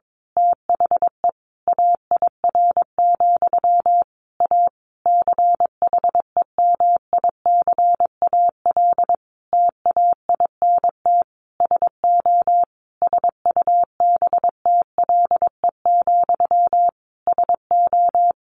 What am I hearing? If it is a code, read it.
Morse code, heard as a keyed tone, about 22 wpm, 700 Hz. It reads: THE AIR, A CHEMICAL TAINT SO SUBTLE, SO
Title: A Sound Of Thunder 22wpm